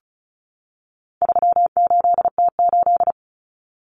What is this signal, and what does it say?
Morse code, heard as a keyed tone, about 34 wpm, 700 Hz. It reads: 38T8
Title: Common Cut Numbers - 4-Digit 35wpm